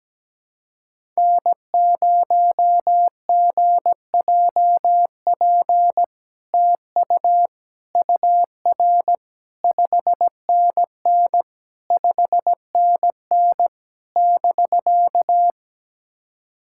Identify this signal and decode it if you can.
Morse code, heard as a keyed tone, about 17 wpm, 700 Hz. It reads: N0GJP TU UR 5NN 5NN <BK>
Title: SOTA QSO Practice 17wpm